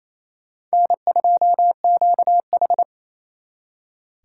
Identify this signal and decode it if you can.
Morse code, heard as a keyed tone, about 35 wpm, 700 Hz. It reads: N2QH